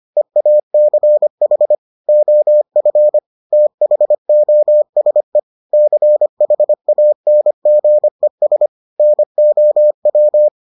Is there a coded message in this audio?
Yes: Morse code from 0.2 to 10.5 s, heard as a keyed tone, about 25 wpm, 600 Hz. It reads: EACH OF THOSE CHANGES NOW